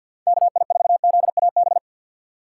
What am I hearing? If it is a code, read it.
Morse code, heard as a keyed tone, about 60 wpm, 700 Hz. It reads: KI4ZAB
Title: Speed Racing - USA 2x3 Call Signs 50wpm